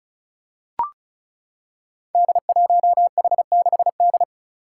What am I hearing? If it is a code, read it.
Morse code, heard as a keyed tone, about 36 wpm, 700 Hz. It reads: D1H6D
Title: Five Challenging Letter-Number Combinations 35wpm